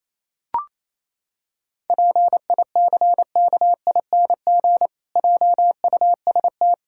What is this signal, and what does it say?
Morse code, heard as a keyed tone, about 28 wpm, 700 Hz. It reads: PICKING JUST
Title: Pangram Challenge v2 - 28wpm